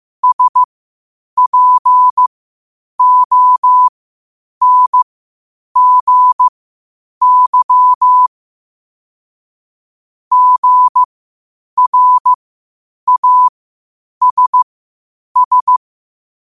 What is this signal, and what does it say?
Morse code, heard as a keyed tone, about 15 wpm, spaced as for 8 wpm, 1000 Hz. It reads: SPONGY GRASS
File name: qotdfiles / 8wpm